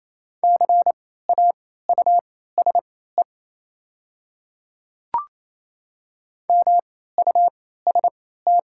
Morse code, heard as a keyed tone, about 27 wpm, spaced as for 16 wpm, 700 Hz. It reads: CAUSE MUST
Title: 3x Character Spacing - Top 200 Words - 28wpm